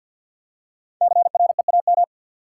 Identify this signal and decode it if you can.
Morse code, heard as a keyed tone, about 53 wpm, 700 Hz. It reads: CREAM
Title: Five Letter Words - 50wpm